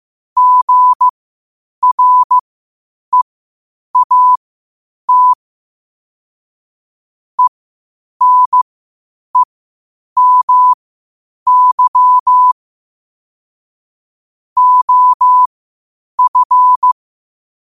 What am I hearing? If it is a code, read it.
Morse code, heard as a keyed tone, about 15 wpm, spaced as for 8 wpm, 1000 Hz. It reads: GREAT ENEMY OF
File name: qotdfiles / 8wpm